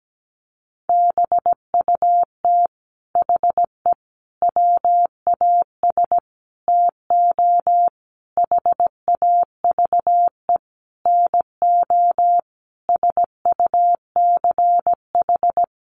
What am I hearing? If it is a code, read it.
Morse code, heard as a keyed tone, about 17 wpm, 700 Hz. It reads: BUT HE WAS TO HAVE NO SUCH
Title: Sentences from Top 1500 Words 17wpm